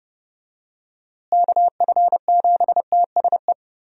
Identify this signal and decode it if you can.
Morse code, heard as a keyed tone, about 30 wpm, 700 Hz. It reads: KF7TSE
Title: USA Call Signs - 6 Characters - Repeated 3x - 30wpm